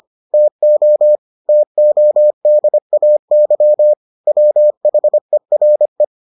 Morse code, heard as a keyed tone, about 25 wpm, 600 Hz. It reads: TO TODAY WHERE